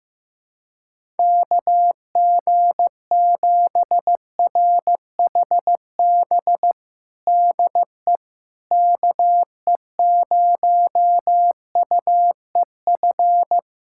Morse code, heard as a keyed tone, about 15 wpm, 700 Hz. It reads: KG7RHB DE KE0UEF
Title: USA Callsign DE Callsign - 6 Characters - 15wpm